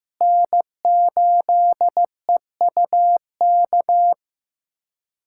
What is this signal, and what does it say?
Morse code, heard as a keyed tone, about 15 wpm, 700 Hz. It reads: N8EUK
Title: USA Callsign DE Callsign - 5 Characters - 15wpm